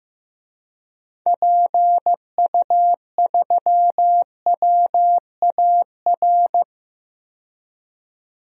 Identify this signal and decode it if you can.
Morse code, heard as a keyed tone, about 15 wpm, 700 Hz. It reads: PU3WAR
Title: DXCC-100 Call Signs - 6 Characters - 15wpm